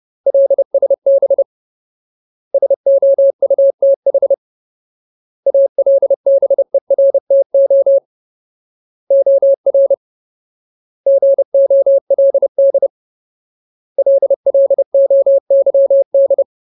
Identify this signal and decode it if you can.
Morse code, heard as a keyed tone, about 30 wpm, 550 Hz. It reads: LSB SOUTH ALBERTO OR GOLD LLOYD
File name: Ultimate Word List / 30WPM Ultimate